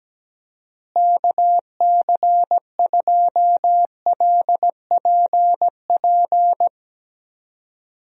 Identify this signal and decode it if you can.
Morse code, heard as a keyed tone, about 17 wpm, 700 Hz. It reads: KC2LPP
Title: USA Call Signs - 6 Characters - Repeated 3x - 17wpm